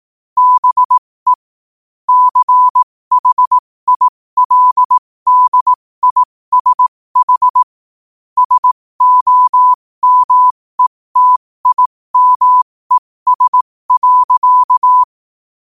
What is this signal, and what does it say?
Morse code, heard as a keyed tone, about 18 wpm, spaced as for 15 wpm, 1000 Hz. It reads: BE CHILDISH SOMETIMES.
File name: qotdfiles / 15wpm